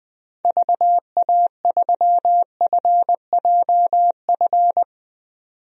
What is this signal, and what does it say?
Morse code, heard as a keyed tone, about 20 wpm, 700 Hz. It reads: VA3FJF
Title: DXCC-100 Call Signs - 6 Characters - 20wpm